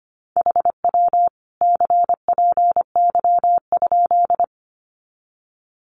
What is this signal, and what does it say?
Morse code, heard as a keyed tone, about 24 wpm, 700 Hz. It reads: HW CPY?
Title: Common QSO Phrases 25wpm